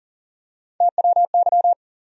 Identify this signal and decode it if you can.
Morse code, heard as a keyed tone, about 40 wpm, 700 Hz. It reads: TWY
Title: USA Call Sign Postfixes - 3 Letters 40wpm